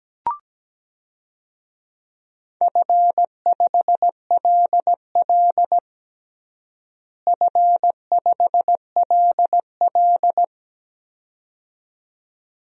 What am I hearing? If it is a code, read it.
Morse code, heard as a keyed tone, about 17 wpm, 700 Hz. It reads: F5LL F5LL
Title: DXCC-100 Call Signs - 4 Characters - Repeated 2x - 17wpm